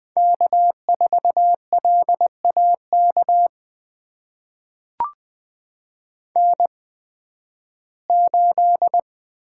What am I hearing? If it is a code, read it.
Morse code, heard as a keyed tone, about 20 wpm, spaced as for 13 wpm, 700 Hz. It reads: K4LAK N8
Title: USA Call Signs - 5 Characters - 8x Character Spacing - 20wpm